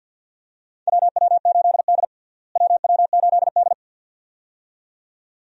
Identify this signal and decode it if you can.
Morse code, heard as a keyed tone, about 52 wpm, 700 Hz. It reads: WW8D WW8D
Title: USA Call Signs - 4 Characters - Repeated 2x - 50wpm